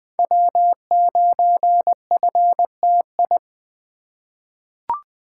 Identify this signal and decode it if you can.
Morse code, heard as a keyed tone, about 21 wpm, 700 Hz. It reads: W9FTI
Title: USA Call Signs - 5 Characters - 3x Character Spacing - 20wpm